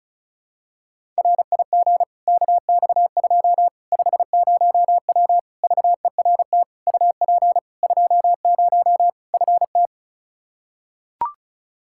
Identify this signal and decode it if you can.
Morse code, heard as a keyed tone, about 36 wpm, 700 Hz. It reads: RIG KX2 50W VERT UP 20 FT
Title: Practice QSO - Exchange 2 - Info - 35wpm